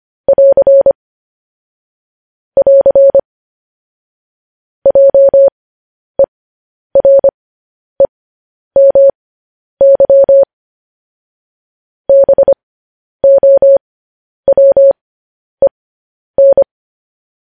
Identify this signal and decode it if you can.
Morse code, heard as a keyed tone, about 25 wpm, spaced as for 10 wpm, 550 Hz. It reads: <AR> <AR> JEREMY BOWEN